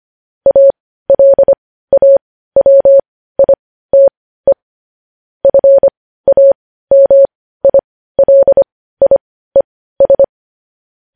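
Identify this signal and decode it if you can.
Morse code, heard as a keyed tone, about 25 wpm, spaced as for 15 wpm, 550 Hz. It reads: ALAWITE FAMILIES
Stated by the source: News headlines in Morse code at 15 words per minute, characters sent at 25 WPM.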